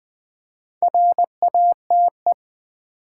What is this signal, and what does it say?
Morse code, heard as a keyed tone, about 20 wpm, 700 Hz. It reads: RATE
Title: Build A Word - 2 Letters At A Time - 4-Letter Words - 20wpm